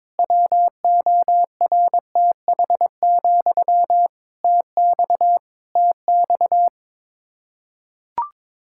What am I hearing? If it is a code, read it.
Morse code, heard as a keyed tone, about 22 wpm, 700 Hz. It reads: WORTH, TX TX
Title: Practice QSO - Exchange 1 - Beginning - 22wpm